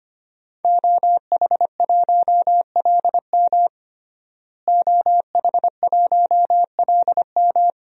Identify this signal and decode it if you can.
Morse code, heard as a keyed tone, about 25 wpm, 700 Hz. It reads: OH1LM OH1LM
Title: DXCC-100 Call Signs - 5 Characters - Repeated 2x - 25wpm